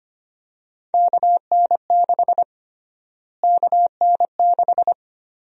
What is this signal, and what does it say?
Morse code, heard as a keyed tone, about 25 wpm, 700 Hz. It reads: KN6 KN6
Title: USA Call Sign Prefixes - Repeated 2x - 25wpm